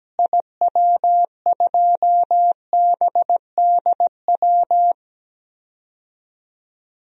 Morse code, heard as a keyed tone, about 17 wpm, 700 Hz. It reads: IW2BDW
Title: DXCC-100 Call Signs - 6 Characters - Repeated 3x - 17wpm